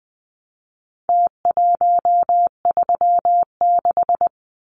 Morse code, heard as a keyed tone, about 20 wpm, 700 Hz. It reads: T136
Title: Numbers - Contest Style - 4-Digit 20wpm